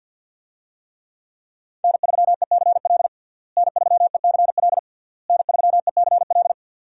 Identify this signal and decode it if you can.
Morse code, heard as a keyed tone, about 47 wpm, 700 Hz. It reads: N3ECL N3ECL N3ECL
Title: USA Call Signs - 5 Characters - Repeated 3x - 50wpm